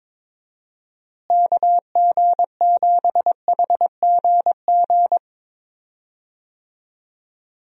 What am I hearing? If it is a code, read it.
Morse code, heard as a keyed tone, about 21 wpm, 700 Hz. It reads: KG7HGG
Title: USA Call Signs - 6 Characters - Repeated 3x - 22wpm